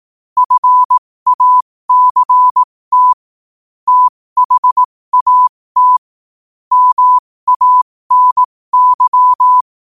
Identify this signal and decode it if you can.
Morse code, heard as a keyed tone, about 18 wpm, spaced as for 15 wpm, 1000 Hz. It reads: FACT THAT MANY
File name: qotdfiles / 15wpm